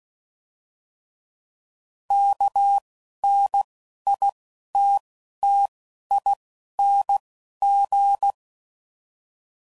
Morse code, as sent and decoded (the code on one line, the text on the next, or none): -.- / -. / .. / - / - / .. / -. / --.
K N I T T I N G